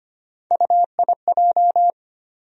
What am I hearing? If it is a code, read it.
Morse code, heard as a keyed tone, about 25 wpm, 700 Hz. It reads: UIJ
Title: USA Call Sign Postfixes - 3 Letters 25wpm